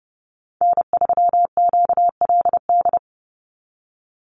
Morse code, heard as a keyed tone, about 30 wpm, 700 Hz. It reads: N3QLD
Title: USA Call Signs 30wpm